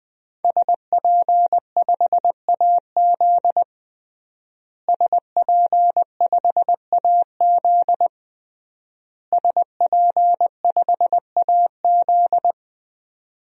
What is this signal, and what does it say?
Morse code, heard as a keyed tone, about 20 wpm, 700 Hz. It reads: SP5AZ SP5AZ SP5AZ
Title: DXCC-100 Call Signs - 5 Characters - Repeated 3x - 20wpm